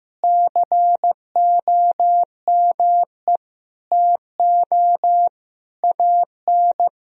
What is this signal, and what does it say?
Morse code, heard as a keyed tone, about 15 wpm, 700 Hz. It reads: COME TO AN
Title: Sentences from Top 1000 Words - Review Part 3 - 15wpm